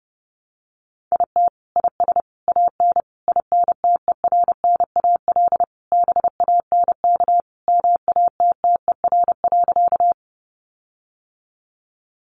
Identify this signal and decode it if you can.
Morse code, heard as a keyed tone, about 30 wpm, 700 Hz. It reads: IT IS AN INTERNAL BANK MATTER.
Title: Sentences from Top 1400 Words 30wpm